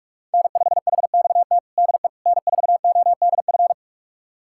Morse code, heard as a keyed tone, about 41 wpm, 700 Hz. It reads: N5HXT DE N4ODF
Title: USA Callsign DE Callsign - 5 Characters - 45wpm